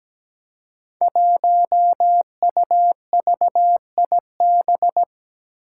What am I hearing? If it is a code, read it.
Morse code, heard as a keyed tone, about 17 wpm, 700 Hz. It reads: 1UVIB
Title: Five Challenging Letter-Number Combinations 17wpm